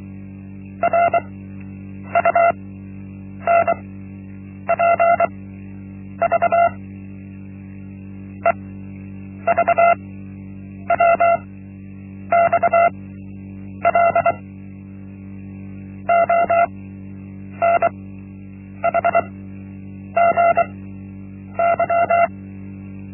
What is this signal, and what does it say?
Morse code, heard as a keyed tone, about 24 wpm, spaced as for 8 wpm, 680 Hz. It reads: RUNPV EVWXL ONHGY
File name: Linksmakalnis-1